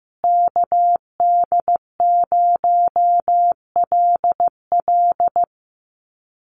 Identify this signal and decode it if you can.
Morse code, heard as a keyed tone, about 14 wpm, 700 Hz. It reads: KD0LL
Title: USA Call Signs Repeated 3x 15wpm